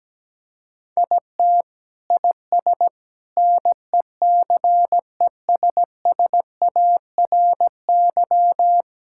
Morse code, heard as a keyed tone, about 17 wpm, 700 Hz. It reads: IT IS NECESSARY
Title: Sentences from Top 1000 Words - Review Part 5 - 17wpm